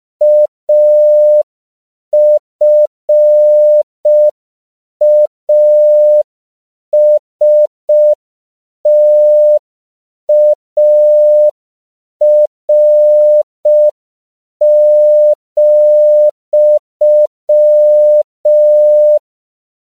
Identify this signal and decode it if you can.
Morse code, heard as a keyed tone, about 5 wpm, 600 Hz. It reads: AFASTAR,
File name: CW / 5PPM